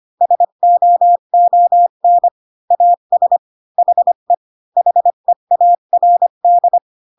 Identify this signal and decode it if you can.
Morse code, heard as a keyed tone, about 25 wpm, 700 Hz. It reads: SOON AS HE HEARD
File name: Pinocchio / Pinocchio ch7 20 25 ik1hge